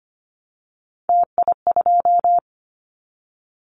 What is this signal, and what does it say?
Morse code, heard as a keyed tone, about 25 wpm, 700 Hz. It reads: TI2
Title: Callsign Prefixes for Top 100 DXCC - 25wpm